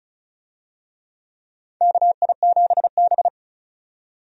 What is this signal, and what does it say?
Morse code, heard as a keyed tone, about 36 wpm, 700 Hz. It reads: KI7B
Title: USA Call Signs - 4 Characters - 35wpm